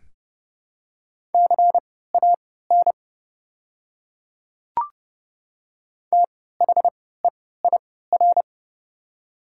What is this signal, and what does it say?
Morse code, heard as a keyed tone, about 30 wpm, spaced as for 17 wpm, 700 Hz. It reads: CAN THEIR
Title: 3x Character Spacing - Top 100 Words - 30wpm